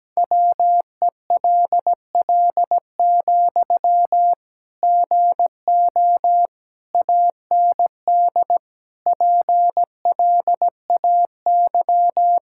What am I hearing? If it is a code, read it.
Morse code, heard as a keyed tone, about 17 wpm, 700 Hz. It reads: WELL, GO AND PLAY
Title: Sentences from Top 1600 Words 17wpm